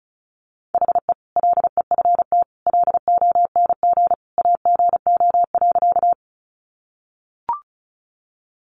Morse code, heard as a keyed tone, about 34 wpm, 700 Hz. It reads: HE LEFT LONG AGO.
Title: Sentences from Top 500 Words 35wpm